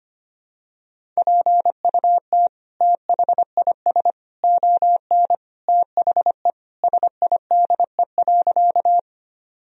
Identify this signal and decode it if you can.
Morse code, heard as a keyed tone, about 25 wpm, 700 Hz. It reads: PUT THIS ON THE SIDE.
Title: Sentences from Top 1000 Words - Review Part 1 - 25wpm